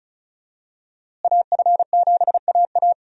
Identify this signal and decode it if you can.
Morse code, heard as a keyed tone, about 34 wpm, 700 Hz. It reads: AF7AA
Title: USA Call Signs - 5 Characters - Repeated 3x - 35wpm